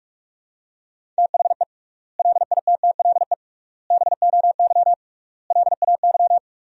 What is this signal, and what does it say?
Morse code, heard as a keyed tone, about 46 wpm, 700 Hz. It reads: THE LITTLE BOY LAY
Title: 3x Word-Spacing - Sentences from Top 500 Words - 45wpm